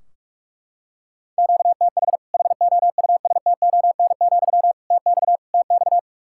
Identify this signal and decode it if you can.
Morse code, heard as a keyed tone, about 42 wpm, 700 Hz. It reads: QTH HOUSTON, TX TX
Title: Practice QSO - Exchange 1 - Beginning - 45wpm